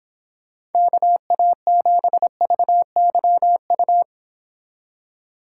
Morse code, heard as a keyed tone, about 29 wpm, 700 Hz. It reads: KA7VYU